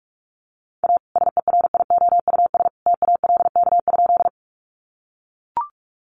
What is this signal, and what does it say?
Morse code, heard as a keyed tone, about 44 wpm, 700 Hz. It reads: A SERIOUS TALK?
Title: Sentences from Top 1100 Words 45wpm